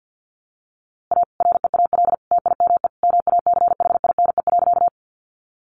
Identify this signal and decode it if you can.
Morse code, heard as a keyed tone, about 50 wpm, 700 Hz. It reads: A REAL TIME MACHINE.
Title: A Sound Of Thunder 50wpm